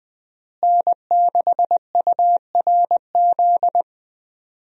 Morse code, heard as a keyed tone, about 20 wpm, 700 Hz. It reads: N6URZ
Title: USA Call Signs - 5 Characters - Repeated 2x - 20wpm